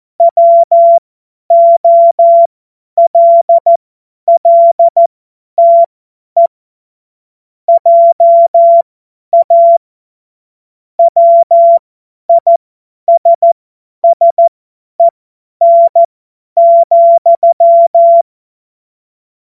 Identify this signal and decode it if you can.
Morse code, heard as a keyed tone, about 14 wpm, spaced as for 10 wpm, 680 Hz. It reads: WOLLTE JA WISSEN,
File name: Morsekurs / Witze / Witz 28. Dezember 2021, 50 F70